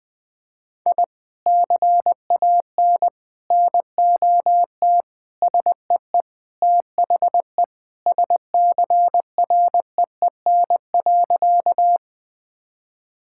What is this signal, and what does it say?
Morse code, heard as a keyed tone, about 21 wpm, 700 Hz. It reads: I CAN NOT SEE THE SCREEN.
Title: Sentences from Top 1000 Words - Review Part 6 - 20wpm